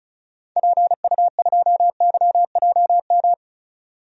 Morse code, heard as a keyed tone, about 36 wpm, 700 Hz. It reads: PU2YJM
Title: DXCC-100 Call Signs - 6 Characters - 35wpm